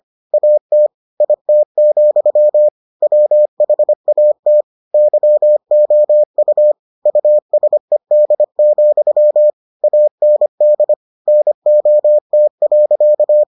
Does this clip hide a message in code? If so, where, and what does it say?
Yes: Morse code from 0.3 to 13.4 s, heard as a keyed tone, about 25 wpm, 600 Hz. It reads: AT IT, WHAT YOU USED, AND NOT.